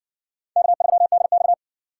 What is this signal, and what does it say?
Morse code, heard as a keyed tone, about 60 wpm, 700 Hz. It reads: K3DX